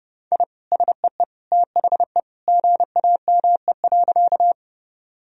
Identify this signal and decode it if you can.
Morse code, heard as a keyed tone, about 32 wpm, 700 Hz. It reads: I SEE THE GAME.
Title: Sentences from Top 1000 Words - Review Part 3 - 30wpm